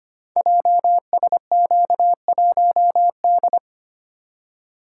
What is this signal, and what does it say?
Morse code, heard as a keyed tone, about 25 wpm, 700 Hz. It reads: JSQ1D
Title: Code Groups - 5 Letter:Numbers - 25wpm